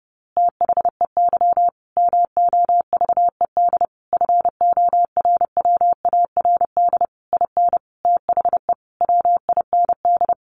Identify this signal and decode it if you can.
Morse code, heard as a keyed tone, about 30 wpm, 700 Hz. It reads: THEY MOVED FORWARD IN THE WIND
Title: A Sound Of Thunder 30wpm